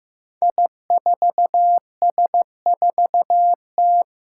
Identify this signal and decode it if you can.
Morse code, heard as a keyed tone, about 15 wpm, 700 Hz. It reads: I4S4T
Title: Five Challenging Letter-Number Combinations 15wpm